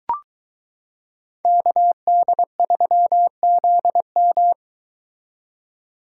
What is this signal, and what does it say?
Morse code, heard as a keyed tone, about 22 wpm, 700 Hz. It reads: KD3ZM